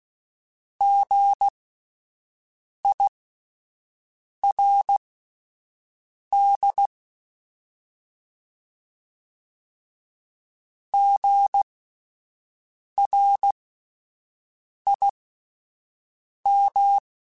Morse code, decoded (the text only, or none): GIRD GRIM